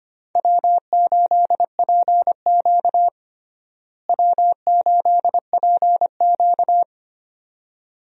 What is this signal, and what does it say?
Morse code, heard as a keyed tone, about 25 wpm, 700 Hz. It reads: W8PQ W8PQ
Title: USA Call Signs - 4 Characters - Repeated 2x - 25wpm